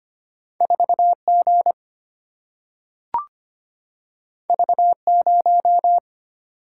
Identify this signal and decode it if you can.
Morse code, heard as a keyed tone, about 25 wpm, 700 Hz. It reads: 4G V0
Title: Single-Letter Single-Number 25wpm